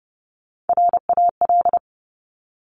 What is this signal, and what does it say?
Morse code, heard as a keyed tone, about 30 wpm, 700 Hz. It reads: RAL
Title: Top Three Letter Combinations in Words 30wpm